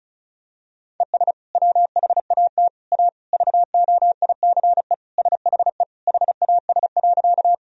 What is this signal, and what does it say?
Morse code, heard as a keyed tone, about 36 wpm, 700 Hz. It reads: ES WHAT A VOICE SHE HAS.
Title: Sentences from Top 500 Words and Abbreviations 35wpm